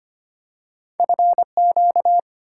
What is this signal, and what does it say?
Morse code, heard as a keyed tone, about 25 wpm, 700 Hz. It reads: FQ
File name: Two Letter-Number Combinations - 25wpm